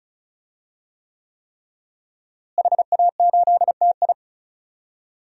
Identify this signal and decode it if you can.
Morse code, heard as a keyed tone, about 34 wpm, 700 Hz. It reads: HA8TI
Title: DXCC-100 Call Signs - 5 Characters - Repeated 3x - 35wpm